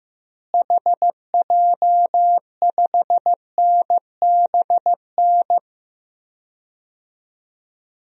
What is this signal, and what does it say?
Morse code, heard as a keyed tone, about 15 wpm, 700 Hz. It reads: HJ5NBN
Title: DXCC-100 Call Signs - 6 Characters - Repeated 2x - 15wpm